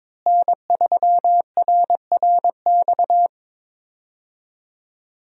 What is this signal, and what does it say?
Morse code, heard as a keyed tone, about 22 wpm, 700 Hz. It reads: N3RRX
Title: USA Call Signs - 5 Characters - Repeated 2x - 22wpm